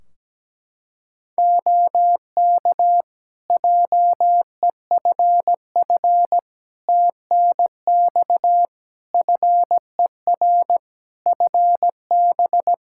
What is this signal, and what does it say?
Morse code, heard as a keyed tone, about 17 wpm, 700 Hz. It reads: OK JEFF TNX FER FB
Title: Practice QSO - Exchange 4 - Ending - 17wpm